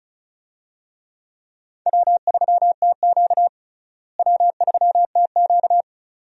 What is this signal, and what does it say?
Morse code, heard as a keyed tone, about 34 wpm, 700 Hz. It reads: W3TQ W3TQ
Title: USA Call Signs - 4 Characters - Repeated 2x - 35wpm